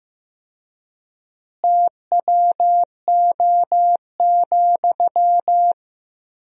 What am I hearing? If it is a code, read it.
Morse code, heard as a keyed tone, about 16 wpm, 700 Hz. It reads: TWO,
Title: 4x Word-Spacing - Sentences from Top 200 Words - 15wpm